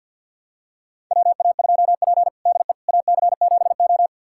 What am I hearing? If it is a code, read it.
Morse code, heard as a keyed tone, about 52 wpm, 700 Hz. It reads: WA2P DE AC7O
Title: USA Callsign DE Callsign - 4 Characters - 50wpm